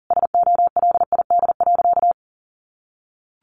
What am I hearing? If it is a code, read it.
Morse code, heard as a keyed tone, about 40 wpm, 700 Hz. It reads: SOLID.
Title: Sentences from Top 1500 Words 40wpm